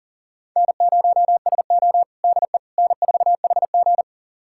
Morse code, heard as a keyed tone, about 37 wpm, 700 Hz. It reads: N0SO DE N4HG
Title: USA Callsign DE Callsign - 4 Characters - 40wpm